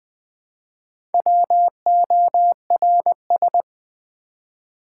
Morse code, heard as a keyed tone, about 20 wpm, 700 Hz. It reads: WORS
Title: Build A Word - 1 Letter At A Time - 5-Letter Words - 20wpm